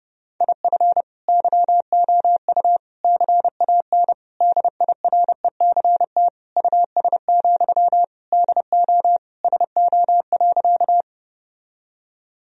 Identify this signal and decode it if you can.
Morse code, heard as a keyed tone, about 32 wpm, 700 Hz. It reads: IF YOU CAN DIRECT US, DO SO.
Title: Sentences from Top 1000 Words - Review Part 2 - 30wpm